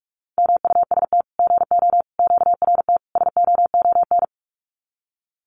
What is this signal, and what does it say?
Morse code, heard as a keyed tone, about 45 wpm, 700 Hz. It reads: MUST GO QRT SOON
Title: Common QSO Phrases 45wpm